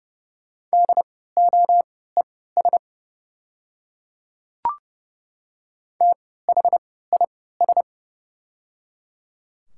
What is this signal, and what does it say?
Morse code, heard as a keyed tone, about 30 wpm, spaced as for 17 wpm, 690 Hz. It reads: DOES THIS
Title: 3x Character Spacing - Top 500 Words - 30wpm